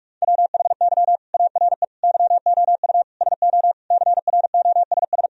Morse code, heard as a keyed tone, about 41 wpm, 700 Hz. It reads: WHY ARE YOU SO CROSS
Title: Sentences from Top 1000 Words - Review Part 1 - 45wpm